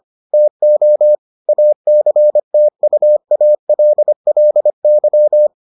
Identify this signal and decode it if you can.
Morse code, heard as a keyed tone, about 25 wpm, 600 Hz. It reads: TO ACTUALLY